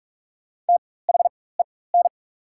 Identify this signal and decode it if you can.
Morse code, heard as a keyed tone, about 43 wpm, spaced as for 21 wpm, 690 Hz. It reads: THEN
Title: 4x Character Spacing - Top 500 Words - 45wpm